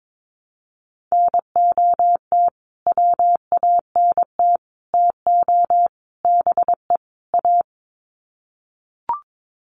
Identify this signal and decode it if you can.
Morse code, heard as a keyed tone, about 22 wpm, 700 Hz. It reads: NOT WANT TO BE A
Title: Sets of 5 Words Encore 22wpm